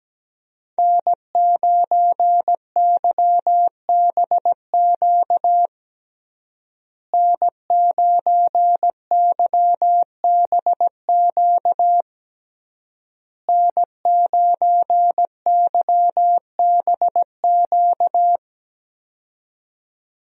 Morse code, heard as a keyed tone, about 17 wpm, 700 Hz. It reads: N9YBQ N9YBQ N9YBQ
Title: USA Call Signs - 5 Characters - Repeated 3x - 17wpm